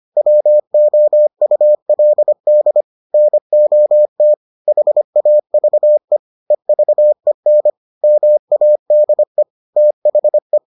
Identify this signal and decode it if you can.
Morse code, heard as a keyed tone, about 25 wpm, 600 Hz. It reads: WOULD NOT HAVE EVEN MADE THE